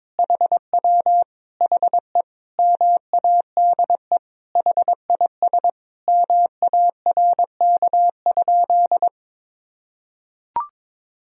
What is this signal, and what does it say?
Morse code, heard as a keyed tone, about 22 wpm, 700 Hz. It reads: HW HE MADE HIS MARK?
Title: Sentences from Top 300 Words and Abbreviations 22wpm